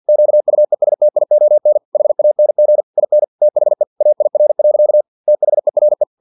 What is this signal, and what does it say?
Morse code, heard as a keyed tone, about 50 wpm, 600 Hz. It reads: QUESTION HANG IN THE AIR. THERE
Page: Never 49wpm